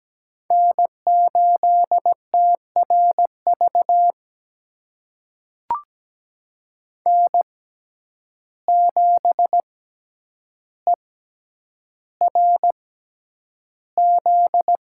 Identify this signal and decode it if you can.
Morse code, heard as a keyed tone, about 17 wpm, spaced as for 9 wpm, 700 Hz. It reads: N8TRV N7ERZ
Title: USA Call Signs - 5 Characters - 6x Character Spacing - 17wpm